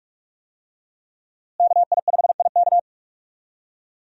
Morse code, heard as a keyed tone, about 46 wpm, 700 Hz. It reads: KI5IK
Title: USA Call Signs - 5 Characters - Repeated 2x - 45wpm